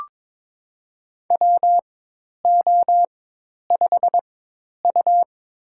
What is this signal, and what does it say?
Morse code, heard as a keyed tone, about 21 wpm, spaced as for 10 wpm, 700 Hz. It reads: WO5U
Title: USA Call Signs - 4 Characters - 4x Character Spacing - 22wpm